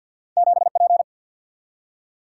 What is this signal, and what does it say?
Morse code, heard as a keyed tone, about 48 wpm, 700 Hz. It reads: 7 P
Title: Single-Letter Single-Number 50wpm